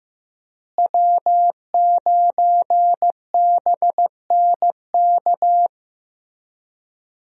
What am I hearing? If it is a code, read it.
Morse code, heard as a keyed tone, about 15 wpm, 700 Hz. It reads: W9BNK
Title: USA Call Signs - 5 Characters - Repeated 2x - 15wpm